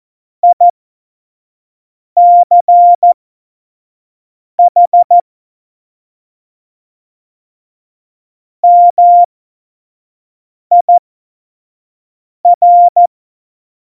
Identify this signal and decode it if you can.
Morse code, heard as a keyed tone, about 14 wpm, spaced as for 5 wpm, 700 Hz. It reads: ICH MIR
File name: Morsekurs / Witze / Witz 26 Mai 2020, 25 F70